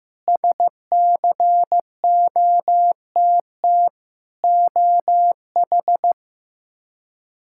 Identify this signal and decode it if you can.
Morse code, heard as a keyed tone, about 15 wpm, 700 Hz. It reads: SCOTT OH
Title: K1USN SST QSO Practice - 15wpm